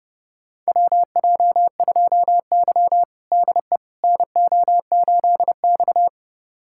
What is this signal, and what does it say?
Morse code, heard as a keyed tone, about 32 wpm, 700 Hz. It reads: WJ2Y DE NO8X
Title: USA Callsign DE Callsign - 4 Characters - 30wpm